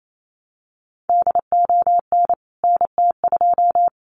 Morse code, heard as a keyed tone, about 28 wpm, 700 Hz. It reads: DON NT2
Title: CWT Exchange 28wpm